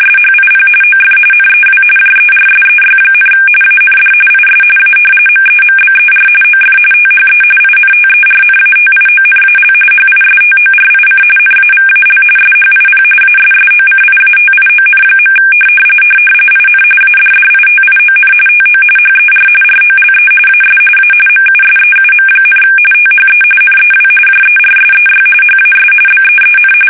RTTY, about 45.45 baud. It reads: SHIP OF NEW JERSEY, AND IS A MEMBER OF THE INTERNATIONAL HISTORIC NAVAL SHIP ASSOCIATION. IN ADDITION TO HER ROLE AS A MUSEUM, THE NEW JERSEY HAS BEEN
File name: fsk850 / battleshipnj